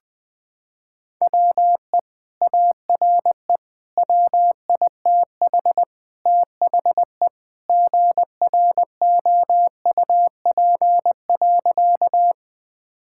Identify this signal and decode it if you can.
Morse code, heard as a keyed tone, about 20 wpm, 700 Hz. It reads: WE ARE WITH THE GROUP.
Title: Sentences with 2-Word Phrases - 20wpm